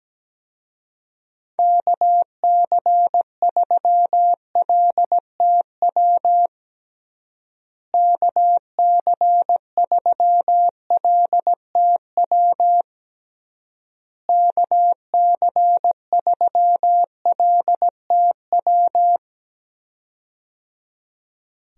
Morse code, heard as a keyed tone, about 17 wpm, 700 Hz. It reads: KC3LTW KC3LTW KC3LTW
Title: USA Call Signs - 6 Characters - Repeated 3x - 17wpm